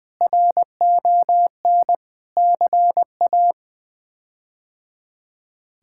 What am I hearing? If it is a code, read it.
Morse code, heard as a keyed tone, about 21 wpm, 700 Hz. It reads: RON CA
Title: K1USN SST QSO Practice - 20wpm